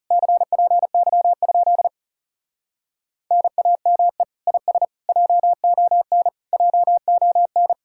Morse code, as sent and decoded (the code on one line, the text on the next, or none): -.-. .--. -.-- ..--.. / -. .- -- . / .. ... / .--- --- -. / .--- --- -.
CPY? NAME IS JON JON